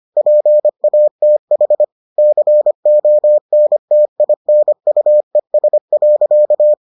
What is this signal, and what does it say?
Morse code, heard as a keyed tone, about 25 wpm, 600 Hz. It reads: PATH CONTINUES.